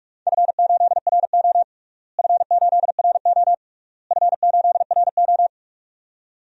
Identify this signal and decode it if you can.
Morse code, heard as a keyed tone, about 47 wpm, 700 Hz. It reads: F8RO F8RO F8RO
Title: DXCC-100 Call Signs - 4 Characters - Repeated 3x - 45wpm